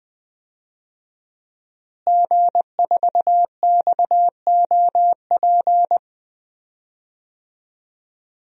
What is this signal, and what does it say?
Morse code, heard as a keyed tone, about 20 wpm, 700 Hz. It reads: G4XOP
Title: DXCC-100 Call Signs - 5 Characters - Repeated 2x - 20wpm